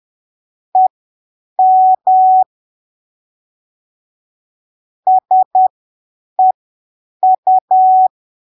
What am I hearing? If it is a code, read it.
Morse code, heard as a keyed tone, about 10 wpm, spaced as for 7 wpm, 750 Hz. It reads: EM SEU
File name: arquivos / texto / texto 28 10ppm